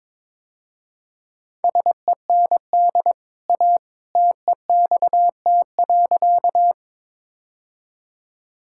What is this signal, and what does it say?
Morse code, heard as a keyed tone, about 22 wpm, 700 Hz. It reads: SEND A TEXT.
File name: practice / 22wpm / Easy Sentences with Hints - 22wpm